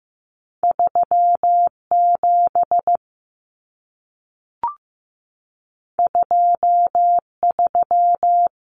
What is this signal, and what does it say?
Morse code, heard as a keyed tone, about 15 wpm, 700 Hz. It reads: 37 23
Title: Numbers - 2-Digit 15wpm